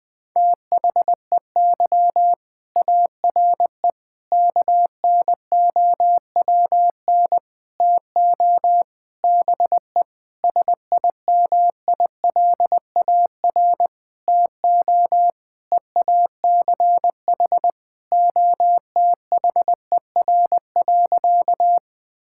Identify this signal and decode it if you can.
Morse code, heard as a keyed tone, about 21 wpm, 700 Hz. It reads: THEY ARE KNOWN TO BE SIMILAR TO EACH OTHER.
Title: Sentences from Top 1000 Words - Review Part 5 - 20wpm